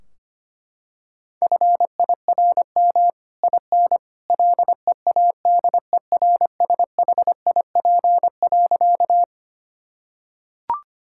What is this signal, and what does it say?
Morse code, heard as a keyed tone, about 25 wpm, 700 Hz. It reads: FIRM IN LEADERSHIP.
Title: Sentences from Top 1600 Words 25wpm